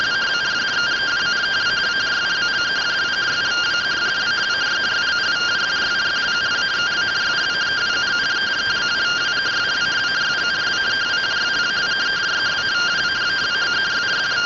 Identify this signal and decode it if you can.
Two-tone teleprinter signal, about 170 Hz shift.